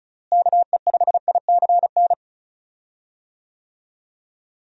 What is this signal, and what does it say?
Morse code, heard as a keyed tone, about 36 wpm, 700 Hz. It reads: KE5ICN
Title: USA Call Signs - 6 Characters - Repeated 2x - 35wpm